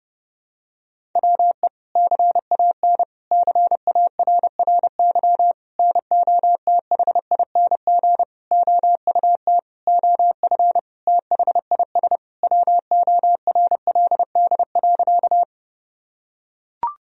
Morse code, heard as a keyed tone, about 29 wpm, 700 Hz. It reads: WE CAN CARRY NOTHING OUT OF THIS WORLD.
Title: Sentences from Top 1000 Words - Review Part 1 - 30wpm